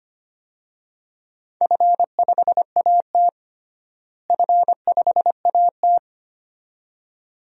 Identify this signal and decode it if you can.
Morse code, heard as a keyed tone, about 25 wpm, 700 Hz. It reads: F5AT F5AT
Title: DXCC-100 Call Signs - 4 Characters - Repeated 2x - 25wpm